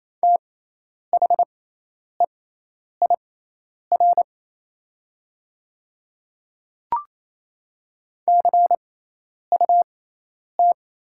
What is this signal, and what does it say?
Morse code, heard as a keyed tone, about 27 wpm, spaced as for 10 wpm, 690 Hz. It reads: THEIR CUT
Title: 6x Character Spacing - Top 300 Words - 28wpm